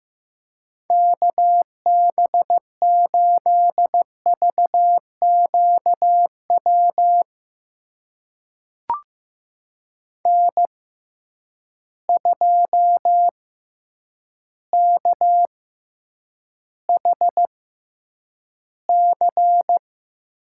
Morse code, heard as a keyed tone, about 15 wpm, spaced as for 6 wpm, 700 Hz. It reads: KB8VQ WN2KHC
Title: USA Call Signs - 6x Character Spacing - 15wpm